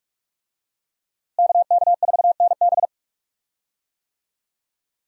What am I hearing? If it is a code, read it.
Morse code, heard as a keyed tone, about 46 wpm, 700 Hz. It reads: KK4NB
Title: USA Call Signs - 5 Characters - Repeated 3x - 45wpm